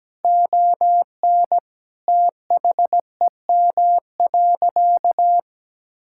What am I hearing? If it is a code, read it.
Morse code, heard as a keyed tone, about 17 wpm, 700 Hz. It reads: ON THEM.
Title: Sentences from Top 1600 Words 17wpm